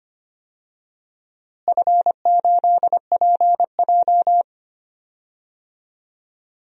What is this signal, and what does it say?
Morse code, heard as a keyed tone, about 25 wpm, 700 Hz. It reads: F8PJ
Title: DXCC-100 Call Signs - 4 Characters - Repeated 2x - 25wpm